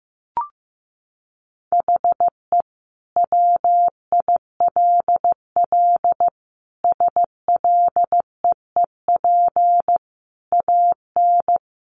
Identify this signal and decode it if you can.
Morse code, heard as a keyed tone, about 15 wpm, 700 Hz. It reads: HE WILL SLEEP AN
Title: Sentences from Top 500 Words 15wpm